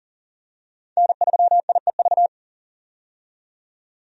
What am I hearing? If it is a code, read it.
Morse code, heard as a keyed tone, about 40 wpm, 700 Hz. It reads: N3IEV
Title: USA Call Signs - 5 Characters - 40wpm